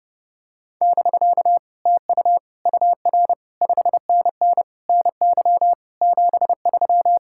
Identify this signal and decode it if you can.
Morse code, heard as a keyed tone, about 30 wpm, 700 Hz. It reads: <BK> TU UR 5NN NY 73